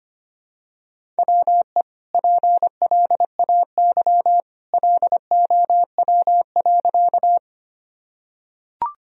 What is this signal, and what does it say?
Morse code, heard as a keyed tone, about 25 wpm, 700 Hz. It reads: WE PLAY LOW.
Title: Sentences from Top 1000 Words - Review Part 1 - 25wpm